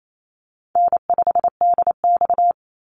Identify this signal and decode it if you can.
Morse code, heard as a keyed tone, about 28 wpm, 700 Hz. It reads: N5DX
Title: Easy Call Signs Repeated 3x - 28wpm